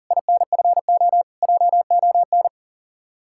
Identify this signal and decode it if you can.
Morse code, heard as a keyed tone, about 48 wpm, 700 Hz. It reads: INFO JON
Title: Speed-Racing Common QSO Phrases - 40wpm